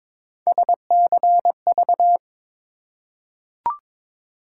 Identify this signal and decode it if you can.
Morse code, heard as a keyed tone, about 23 wpm, 700 Hz. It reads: SCV
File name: ARRL Sections v2 - 22wpm